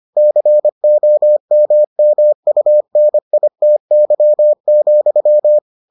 Morse code, heard as a keyed tone, about 25 wpm, 600 Hz. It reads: COMMUNITY,